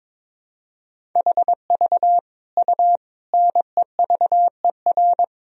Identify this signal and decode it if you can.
Morse code, heard as a keyed tone, about 22 wpm, 700 Hz. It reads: HV U NEVER
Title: Sentences from Top 300 Words and Abbreviations 22wpm